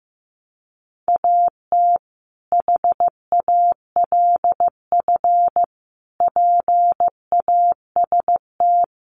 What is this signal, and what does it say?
Morse code, heard as a keyed tone, about 15 wpm, 700 Hz. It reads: AT HALF PAST
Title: Sentences from Top 600 Words 15wpm